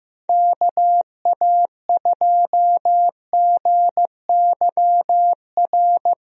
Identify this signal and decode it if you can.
Morse code, heard as a keyed tone, about 15 wpm, 700 Hz. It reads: KA2GYR
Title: USA Call Signs - 6 Characters - Repeated 3x - 15wpm